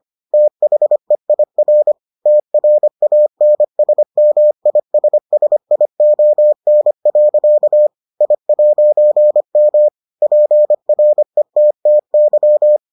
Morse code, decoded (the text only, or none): THEIR TRANSMISSION. I'M PRETTY